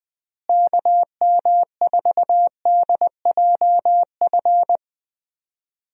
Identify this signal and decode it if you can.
Morse code, heard as a keyed tone, about 20 wpm, 700 Hz. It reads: KM4DJF
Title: USA Call Signs - 6 Characters - Repeated 2x - 20wpm